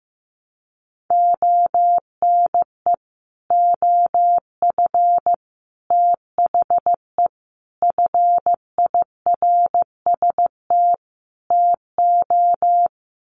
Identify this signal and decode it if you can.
Morse code, heard as a keyed tone, about 15 wpm, 700 Hz. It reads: ONE OF THE FIRST TO
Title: Sets of 5 Words Encore 15wpm